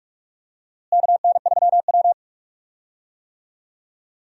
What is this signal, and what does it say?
Morse code, heard as a keyed tone, about 46 wpm, 700 Hz. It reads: KN3W
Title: USA Call Signs - 4 Characters - Repeated 2x - 45wpm